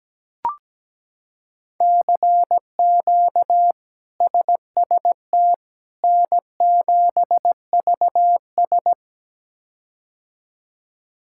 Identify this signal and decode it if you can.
Morse code, heard as a keyed tone, about 17 wpm, 700 Hz. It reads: CQ SST N7VS
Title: K1USN SST QSO Practice - 17wpm